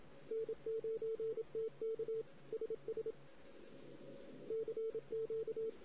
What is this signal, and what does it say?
Morse code, heard as a keyed tone, about 27 wpm, 450 Hz. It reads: N9TK SS CQ
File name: CW / 2008-ss-cw-k5tr-op-n5rz -16:00